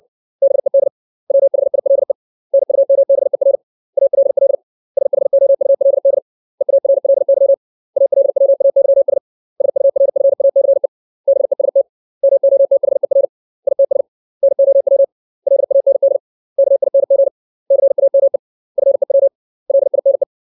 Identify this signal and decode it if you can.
Morse code, heard as a keyed tone, about 60 wpm, 550 Hz. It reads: BED WHILE NUMBER ALL SHOULD EARLY ALWAYS SURFACE BEST MOTHER ITS NOW LAND KING CAME FEW LINE
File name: 632 words / 632 Words 60WPM